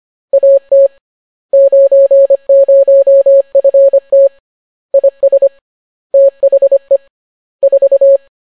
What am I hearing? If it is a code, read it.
Morse code, heard as a keyed tone, about 25 wpm, 550 Hz. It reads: AT 90FT IS THE 4